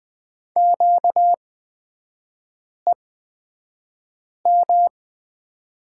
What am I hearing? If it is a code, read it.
Morse code, heard as a keyed tone, about 20 wpm, spaced as for 5 wpm, 690 Hz. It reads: QEM
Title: 236 Lesson 38e - Letter q - Three Characters 20wpm